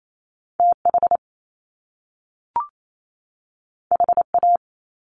Morse code, heard as a keyed tone, about 28 wpm, 700 Hz. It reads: TH HA
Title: Top Two Letter Combinations in Words 28wpm